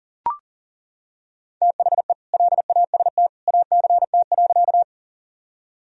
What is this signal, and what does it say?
Morse code, heard as a keyed tone, about 40 wpm, 700 Hz. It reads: THE LAST ACT.
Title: Sentences from Top 1000 Words - Review Part 1 - 40wpm